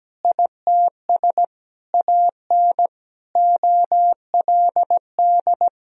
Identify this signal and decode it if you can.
Morse code, heard as a keyed tone, about 17 wpm, 700 Hz. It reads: ITS AN OLD
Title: Sentences from Top 1000 Words - Review Part 2 - 17wpm